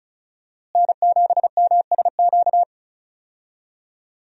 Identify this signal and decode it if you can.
Morse code, heard as a keyed tone, about 36 wpm, 700 Hz. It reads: N7MSQ
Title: USA Call Signs - 5 Characters - Repeated 3x - 35wpm